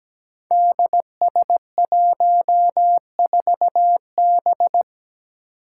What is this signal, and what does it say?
Morse code, heard as a keyed tone, about 17 wpm, 700 Hz. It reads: DS14B
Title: Five Challenging Letter-Number Combinations 17wpm